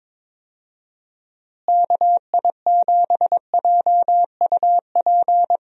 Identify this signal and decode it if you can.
Morse code, heard as a keyed tone, about 22 wpm, 700 Hz. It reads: KI7JUP
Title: USA Call Signs - 6 Characters - Repeated 2x - 22wpm